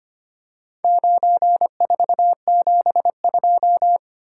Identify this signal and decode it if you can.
Morse code, heard as a keyed tone, about 25 wpm, 690 Hz. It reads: 9472
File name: practice / 25wpm / Four Letter-Number Combinations - 25wpm